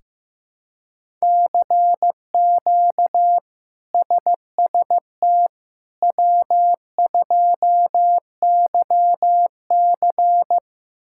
Morse code, heard as a keyed tone, about 15 wpm, 700 Hz. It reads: CQ SST W2YC
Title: K1USN SST QSO Practice - 15wpm